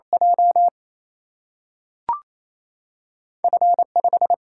Morse code, heard as a keyed tone, about 28 wpm, 700 Hz. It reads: J F 5
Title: Single-Letter Single-Number 28wpm